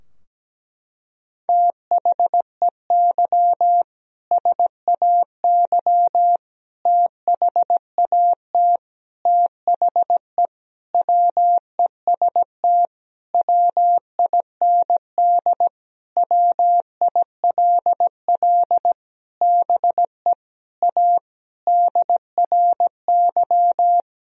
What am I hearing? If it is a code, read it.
Morse code, heard as a keyed tone, about 17 wpm, 700 Hz. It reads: THEY SAY THAT THE WEST WIND WILL BE A DRY
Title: Sentences from Top 1000 Words - Review Part 3 - 17wpm